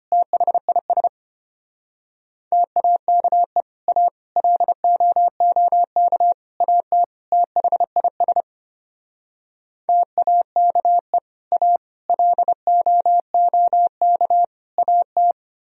Morse code, heard as a keyed tone, about 33 wpm, 700 Hz. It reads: THIS TAKE A LOOK AT THIS TAKE A LOOK AT